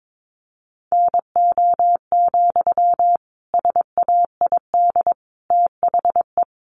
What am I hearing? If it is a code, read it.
Morse code, heard as a keyed tone, about 22 wpm, 700 Hz. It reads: NO, SAID THE
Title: Sentences from Top 1200 Words 22wpm